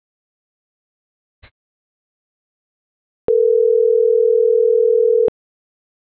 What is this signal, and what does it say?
Ringback tone (North America), 440 + 480 Hz.